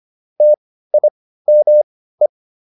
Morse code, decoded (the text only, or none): TIME